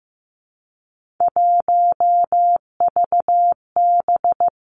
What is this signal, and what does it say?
Morse code, heard as a keyed tone, about 15 wpm, 700 Hz. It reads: 1VB
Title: Three Challenging Letter-Number Combinations 15wpm